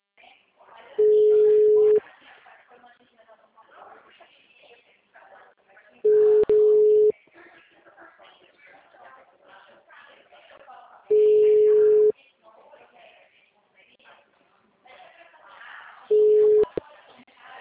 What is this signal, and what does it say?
Ringback tone (Europe, 425 Hz), with speech.
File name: out- -2030- - - .11419